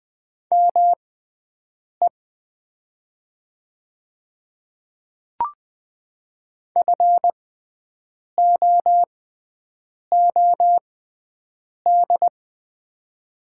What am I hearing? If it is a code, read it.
Morse code, heard as a keyed tone, about 20 wpm, spaced as for 7 wpm, 700 Hz. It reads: ME FOOD
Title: 6x Character Spacing - Top 500 Words - 20wpm